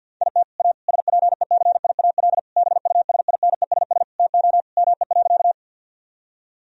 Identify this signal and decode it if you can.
Morse code, heard as a keyed tone, about 50 wpm, 700 Hz. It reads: IT A SPECIAL BUSINESS TO ME.
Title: Sentences from Top 1000 Words - Review Part 3 - 50wpm